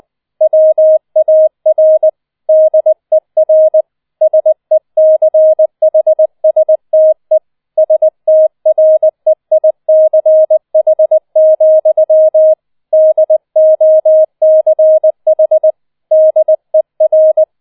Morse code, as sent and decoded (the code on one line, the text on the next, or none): .-- .- .-. / -.. .-. / ... -.-. .... ... - . / ... - .-. .. -.-. .... --..-- / -.. --- -.-. .... / -.. .-.
WAR DER SECHSTE STREICH, DOCH DER